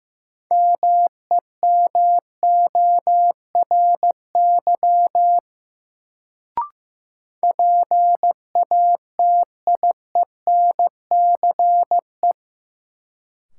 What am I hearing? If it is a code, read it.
Morse code, heard as a keyed tone, about 15 wpm, 700 Hz. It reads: MEMORY PATIENCE
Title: Sentences from Top 1300 Words - New Word Drill Alt - 15wpm